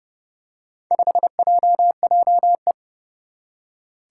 Morse code, heard as a keyed tone, about 30 wpm, 690 Hz. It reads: 5JJE
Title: Four Challenging Letter-Number Combinations 30wpm